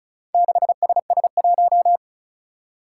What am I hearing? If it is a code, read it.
Morse code, heard as a keyed tone, about 36 wpm, 700 Hz. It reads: 6SS1
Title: Four Challenging Letter-Number Combinations 35wpm